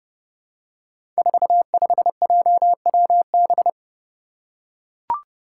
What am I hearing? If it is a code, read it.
Morse code, heard as a keyed tone, about 30 wpm, 700 Hz. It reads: 45JWB
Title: Five Challenging Letter-Number Combinations 30wpm